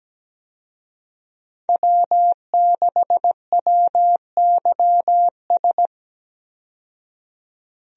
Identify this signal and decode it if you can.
Morse code, heard as a keyed tone, about 17 wpm, 700 Hz. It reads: W6WYS
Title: USA Call Signs - 5 Characters - 17wpm